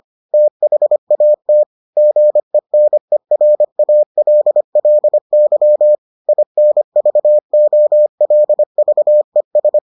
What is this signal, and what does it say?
Morse code, heard as a keyed tone, about 25 wpm, 600 Hz. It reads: THAT GENERALLY INVOLVES